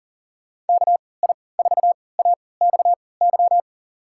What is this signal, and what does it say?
Morse code, heard as a keyed tone, about 40 wpm, spaced as for 22 wpm, 700 Hz. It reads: KI4AXY
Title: USA Call Signs - 6 Characters - 3x Character Spacing - 40wpm